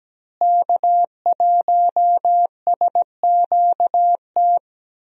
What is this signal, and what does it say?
Morse code, heard as a keyed tone, about 17 wpm, 700 Hz. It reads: K1SQT
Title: USA Call Signs - 5 Characters - Repeated 3x - 17wpm